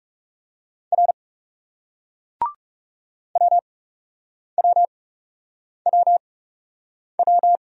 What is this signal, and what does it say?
Morse code, heard as a keyed tone, about 34 wpm, spaced as for 8 wpm, 700 Hz. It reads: R WWWW
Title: Speed-Racing Single Letters 30wpm